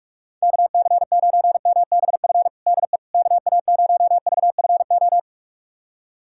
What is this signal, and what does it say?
Morse code, heard as a keyed tone, about 47 wpm, 700 Hz. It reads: KC9MBF DE KA0VFO
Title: USA Callsign DE Callsign - 6 Characters - 45wpm